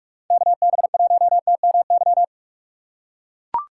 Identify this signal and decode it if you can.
Morse code, heard as a keyed tone, about 44 wpm, 700 Hz. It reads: KB1TMY